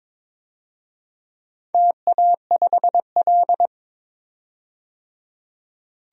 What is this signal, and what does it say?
Morse code, heard as a keyed tone, about 22 wpm, 700 Hz. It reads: TA5L
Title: DXCC-100 Call Signs - 4 Characters - 22wpm